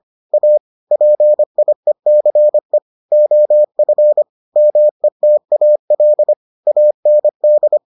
Morse code, heard as a keyed tone, about 25 wpm, 600 Hz. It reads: A PIECE OF METAL AND